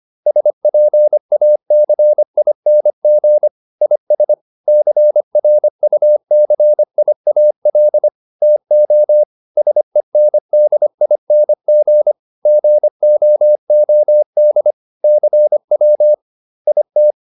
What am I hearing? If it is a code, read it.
Morse code, heard as a keyed tone, about 25 wpm, 600 Hz. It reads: SPACING IS CRUCIAL TO SENDING GOOD CW IT